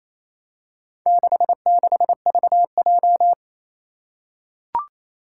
Morse code, heard as a keyed tone, about 27 wpm, 700 Hz. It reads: 66VJ
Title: Four Challenging Letter-Number Combinations 28wpm